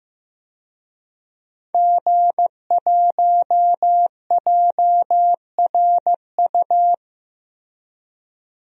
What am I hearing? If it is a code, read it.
Morse code, heard as a keyed tone, about 15 wpm, 700 Hz. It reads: G1JRU
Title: DXCC-100 Call Signs - 5 Characters - 15wpm